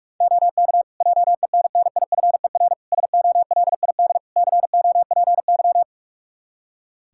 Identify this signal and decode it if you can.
Morse code, heard as a keyed tone, about 46 wpm, 700 Hz. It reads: OK JENNIFER SOLID COPY
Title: Practice QSO - Exchange 3 - More Info - 45wpm